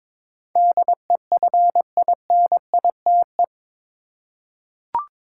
Morse code, heard as a keyed tone, about 22 wpm, 700 Hz. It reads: DEFINITE
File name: practice / 22wpm / Eight Letter Words - 22wpm